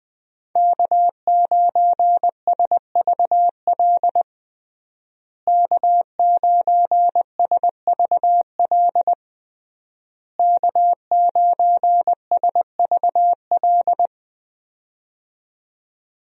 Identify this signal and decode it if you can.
Morse code, heard as a keyed tone, about 20 wpm, 700 Hz. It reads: K9SVL K9SVL K9SVL
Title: USA Call Signs - 5 Characters - Repeated 3x - 20wpm